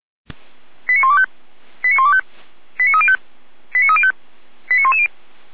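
Five-tone selective call (ZVEI), heard as quick runs of five tones, tones of about 70 ms. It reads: 89125 89125 89305 89305 89110
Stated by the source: ZVEI five-tone selective call received on an SDR.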